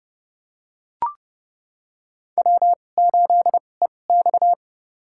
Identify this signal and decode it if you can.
Morse code, heard as a keyed tone, about 32 wpm, spaced as for 22 wpm, 700 Hz. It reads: W8EX
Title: USA Call Signs - 4 Characters - 2x Character Spacing - 30wpm